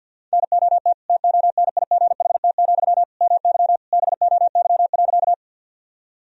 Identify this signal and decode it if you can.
Morse code, heard as a keyed tone, about 53 wpm, 700 Hz. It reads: NOT TONIGHT, MY BOY.
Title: Sentences from Top 1000 Words - Review Part 3 - 50wpm